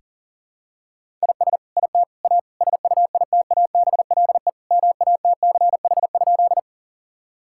Morse code, heard as a keyed tone, about 40 wpm, 700 Hz. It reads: IS IT A SUITABLE MATCH?
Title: Sentences from Top 1600 Words 40wpm